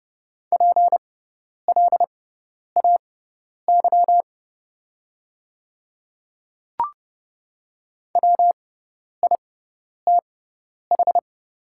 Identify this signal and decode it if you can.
Morse code, heard as a keyed tone, about 30 wpm, spaced as for 10 wpm, 700 Hz. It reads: PLAY WITH
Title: 6x Character Spacing - Top 300 Words - 30wpm